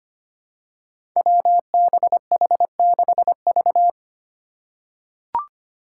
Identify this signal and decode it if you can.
Morse code, heard as a keyed tone, about 25 wpm, 700 Hz. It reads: WBH6V
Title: Five Challenging Letter-Number Combinations 25wpm